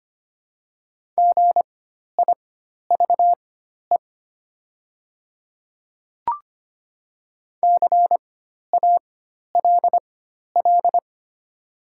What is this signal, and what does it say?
Morse code, heard as a keyed tone, about 25 wpm, spaced as for 12 wpm, 700 Hz. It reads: GIVE CALL
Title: 4x Character Spacing - Top 200 Words - 25wpm